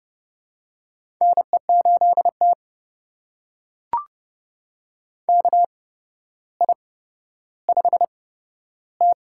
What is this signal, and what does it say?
Morse code, heard as a keyed tone, about 30 wpm, spaced as for 13 wpm, 700 Hz. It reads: NE8T KI5T
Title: USA Call Signs - 4 Characters - 8x Character Spacing - 30wpm